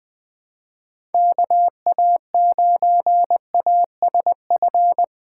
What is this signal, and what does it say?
Morse code, heard as a keyed tone, about 20 wpm, 700 Hz. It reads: KA9ASF
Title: USA Callsign DE Callsign 20wpm